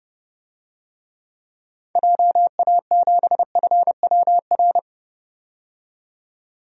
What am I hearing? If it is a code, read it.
Morse code, heard as a keyed tone, about 30 wpm, 700 Hz. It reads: JA7FWR
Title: DXCC-100 Call Signs - 6 Characters - 30wpm